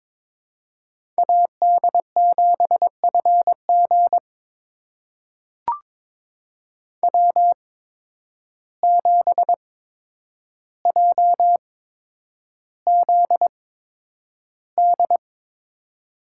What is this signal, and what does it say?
Morse code, heard as a keyed tone, about 22 wpm, spaced as for 9 wpm, 700 Hz. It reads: AD7FG W7JZD
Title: USA Call Signs - 5 Characters - 8x Character Spacing - 22wpm